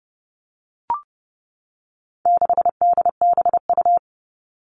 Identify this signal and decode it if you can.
Morse code, heard as a keyed tone, about 30 wpm, 700 Hz. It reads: 6DBU
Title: All Cut Numbers - 4-Digit 30wpm